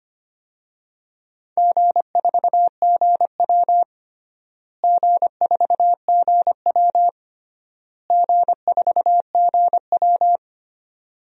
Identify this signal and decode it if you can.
Morse code, heard as a keyed tone, about 25 wpm, 700 Hz. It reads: G4GW G4GW G4GW
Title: DXCC-100 Call Signs - 4 Characters - Repeated 3x - 25wpm